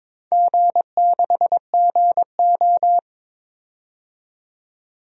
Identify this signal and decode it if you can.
Morse code, heard as a keyed tone, about 23 wpm, 700 Hz. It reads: G6GO
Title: DXCC-100 Call Signs - 4 Characters - 22wpm